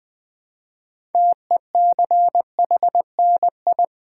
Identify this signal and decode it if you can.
Morse code, heard as a keyed tone, about 20 wpm, 700 Hz. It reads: TECHNI
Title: Build A Word - 3 Letters At A Time - 9-Letter Words - 20wpm